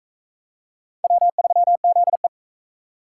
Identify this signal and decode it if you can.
Morse code, heard as a keyed tone, about 38 wpm, 700 Hz. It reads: W3ZE